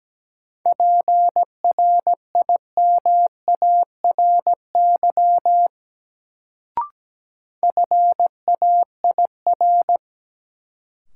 Morse code, heard as a keyed tone, about 17 wpm, 700 Hz. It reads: PRIMARY FAIR
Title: Sentences from Top 1000 Words - Review Part 5 - New Word Drill - 17wpm